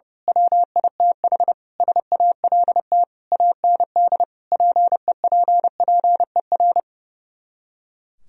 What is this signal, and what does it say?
Morse code, heard as a keyed tone, about 28 wpm, 700 Hz. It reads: WITH SALT AND PEPPER
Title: Speed-Racing - Sets of 5 Words 30wpm